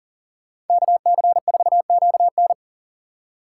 Morse code, heard as a keyed tone, about 48 wpm, 700 Hz. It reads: KC4QN
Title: Speed Racing - USA Call Signs 40wpm